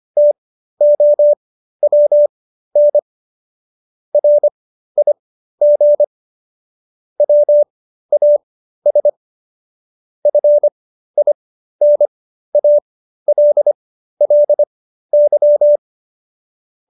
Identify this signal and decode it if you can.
Morse code, heard as a keyed tone, about 25 wpm, spaced as for 13 wpm, 590 Hz. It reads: TOWN RIG WAS FINALLY